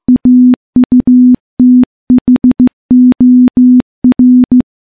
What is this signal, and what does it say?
Morse code, heard as a keyed tone, about 15 wpm, 250 Hz. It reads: AUTHOR
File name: morse-training-15WPM